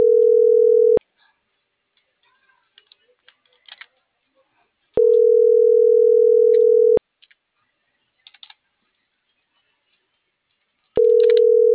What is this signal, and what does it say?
Ringback tone (North America), 440 + 480 Hz.